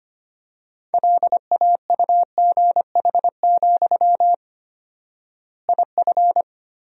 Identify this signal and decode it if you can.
Morse code, heard as a keyed tone, about 25 wpm, 700 Hz. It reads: LAUGH, IF
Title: 4x Word-Spacing - Sentences from Top 500 Words - 25wpm